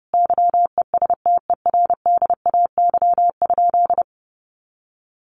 Morse code, heard as a keyed tone, about 30 wpm, 700 Hz. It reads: YESTERDAY?
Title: A Sound Of Thunder 30wpm